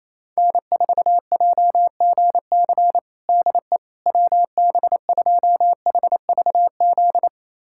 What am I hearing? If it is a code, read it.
Morse code, heard as a keyed tone, about 28 wpm, 700 Hz. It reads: N4JGC DE WB2HVZ
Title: USA Callsign DE Callsign 28wpm